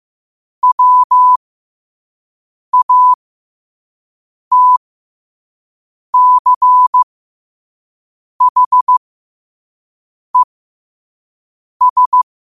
Morse code, heard as a keyed tone, about 15 wpm, spaced as for 5 wpm, 1000 Hz. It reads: WATCHES
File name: qotdfiles / 5wpm